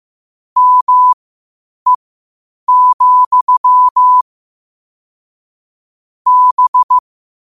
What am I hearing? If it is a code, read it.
Morse code, heard as a keyed tone, about 15 wpm, spaced as for 8 wpm, 1000 Hz. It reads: ME, B
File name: qotdfiles / 8wpm